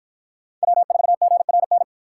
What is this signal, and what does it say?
Morse code, heard as a keyed tone, about 53 wpm, 700 Hz. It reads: W4GRN